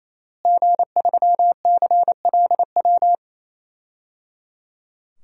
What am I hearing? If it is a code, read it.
Morse code, heard as a keyed tone, about 28 wpm, 700 Hz. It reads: G3CLW
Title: DXCC-100 Call Signs - 5 Characters - Repeated 3x - 28wpm